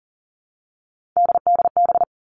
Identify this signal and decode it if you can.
Morse code, heard as a keyed tone, about 40 wpm, 700 Hz. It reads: DDB
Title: All Cut Numbers - 3-Digit 40wpm